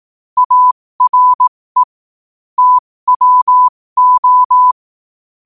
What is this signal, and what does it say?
Morse code, heard as a keyed tone, about 18 wpm, spaced as for 15 wpm, 1000 Hz. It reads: ARE TWO
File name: qotdfiles / 15wpm